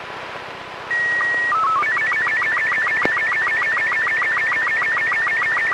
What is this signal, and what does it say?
SSTV (slow-scan television), Robot 36.
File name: ARISSAT-1 2 29 08 2011 3 patrick